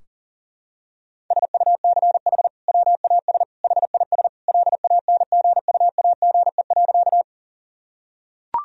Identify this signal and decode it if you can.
Morse code, heard as a keyed tone, about 40 wpm, 700 Hz. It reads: SUCH WAS HIS LANGUAGE.
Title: Sentences from Top 1000 Words - Review Part 3 - 40wpm